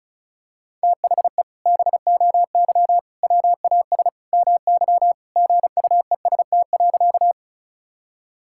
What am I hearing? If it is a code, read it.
Morse code, heard as a keyed tone, about 36 wpm, 700 Hz. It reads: THE BOY WAS MY GUEST.
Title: Sentences from Top 1600 Words 35wpm